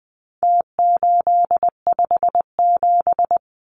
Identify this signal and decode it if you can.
Morse code, heard as a keyed tone, about 20 wpm, 700 Hz. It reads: T857
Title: Numbers - Contest Style - 4-Digit 20wpm